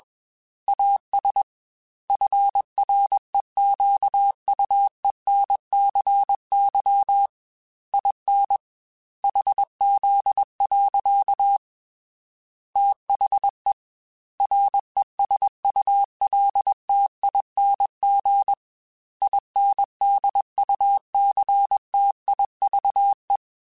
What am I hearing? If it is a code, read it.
Morse code, heard as a keyed tone, about 21 wpm, 800 Hz. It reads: AS FREQUENCY IN HZ. THE RESULTING INDUCTIVE